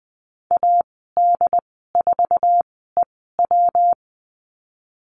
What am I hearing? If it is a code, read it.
Morse code, heard as a keyed tone, about 20 wpm, spaced as for 15 wpm, 700 Hz. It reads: AD4EW
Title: 128 Lesson 21g - Number 6 - Call Signs 20wpm